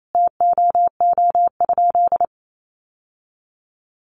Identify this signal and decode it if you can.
Morse code, heard as a keyed tone, about 28 wpm, 700 Hz. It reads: TOO?
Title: Sentences from Top 1400 Words 28wpm